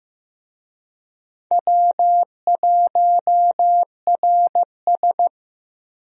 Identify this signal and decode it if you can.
Morse code, heard as a keyed tone, about 15 wpm, 700 Hz. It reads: W1RS
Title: USA Call Signs - 4 Characters - Repeated 3x - 15wpm